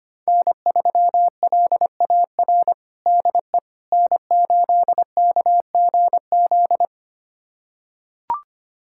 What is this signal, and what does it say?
Morse code, heard as a keyed tone, about 25 wpm, 700 Hz. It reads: N3LAR DE N8KGZ
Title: USA Callsign DE Callsign - 5 Characters - 25wpm